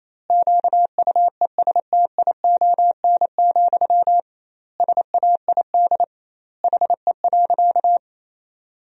Morse code, heard as a keyed tone, about 28 wpm, 700 Hz. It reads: QUESTION, SAID HE.
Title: 2x Word-Spacing - Sentences from Top 400 Words - 28wpm